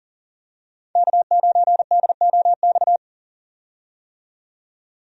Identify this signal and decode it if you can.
Morse code, heard as a keyed tone, about 40 wpm, 700 Hz. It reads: K9DOX
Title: USA Call Signs - 5 Characters - 40wpm